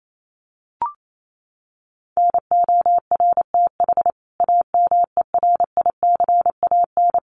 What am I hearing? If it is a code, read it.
Morse code, heard as a keyed tone, about 28 wpm, 700 Hz. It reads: NORTH AMERICAN
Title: Sets of 5 Words Encore 28wpm